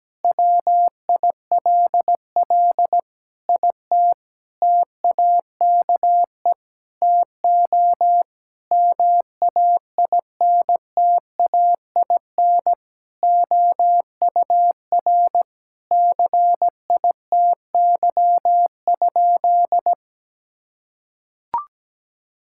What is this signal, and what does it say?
Morse code, heard as a keyed tone, about 17 wpm, 700 Hz. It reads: WILL IT TAKE TO MAINTAIN OUR CITY?
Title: Sentences from Top 1000 Words - Review Part 5 - 17wpm